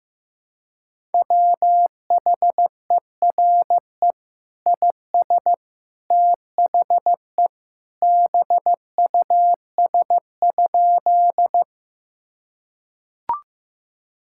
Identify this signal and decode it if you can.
Morse code, heard as a keyed tone, about 15 wpm, 700 Hz. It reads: WHERE IS THE BUS?
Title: Sentences from Top 1000 Words - Review Part 6 - 15wpm